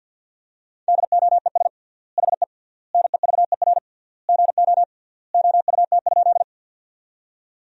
Morse code, heard as a keyed tone, about 48 wpm, 700 Hz. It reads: DOES HE NEVER GO OUT?
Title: 3x Word-Spacing - Sentences from Top 300 Words - 50wpm